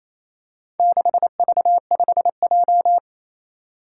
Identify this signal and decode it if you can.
Morse code, heard as a keyed tone, about 28 wpm, 700 Hz. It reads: 6V5J
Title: Four Challenging Letter-Number Combinations 28wpm